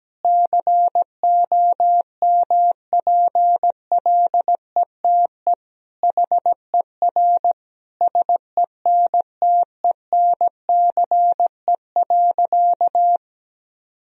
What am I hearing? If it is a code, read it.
Morse code, heard as a keyed tone, about 17 wpm, 700 Hz. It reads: COMPLETE HER SENTENCE.
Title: Sentences from Top 1000 Words - Review Part 2 - 17wpm